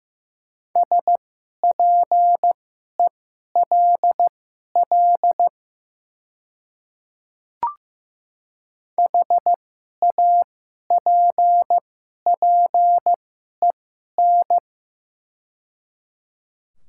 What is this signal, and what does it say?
Morse code, heard as a keyed tone, about 15 wpm, spaced as for 11 wpm, 690 Hz. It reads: SPELL HAPPEN
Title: 2x Character Spacing - Top 500 Words - 15wpm